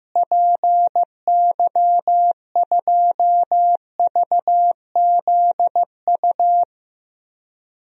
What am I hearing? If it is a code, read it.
Morse code, heard as a keyed tone, about 15 wpm, 700 Hz. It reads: PY2VZU
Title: DXCC-100 Call Signs - 6 Characters - Repeated 3x - 15wpm